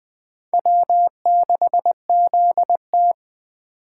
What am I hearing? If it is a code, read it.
Morse code, heard as a keyed tone, about 20 wpm, 700 Hz. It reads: W6ZT
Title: Speed Racing - USA 1x2 Call Signs 17wpm